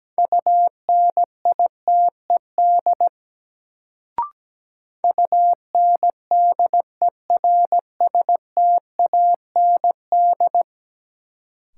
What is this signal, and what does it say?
Morse code, heard as a keyed tone, about 17 wpm, 700 Hz. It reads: UNITED UNDERSTAND
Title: Sentences from Top 600 Words - New Word Drill - 17wpm